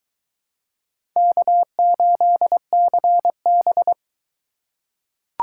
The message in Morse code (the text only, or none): K8CB